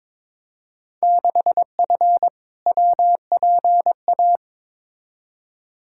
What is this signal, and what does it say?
Morse code, heard as a keyed tone, about 22 wpm, 700 Hz. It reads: 6F WPA
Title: Field Day Exchanges Encore - 22wpm